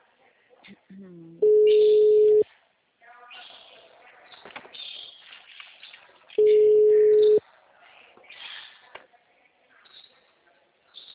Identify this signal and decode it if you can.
Ringback tone (Europe, 425 Hz), with speech.